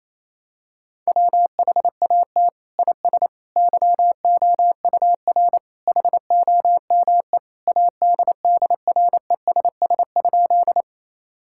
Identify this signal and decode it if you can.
Morse code, heard as a keyed tone, about 27 wpm, 700 Hz. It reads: WHAT IS YOUR HOME ADDRESS?
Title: Sentences from Top 1000 Words - Review Part 6 - 28wpm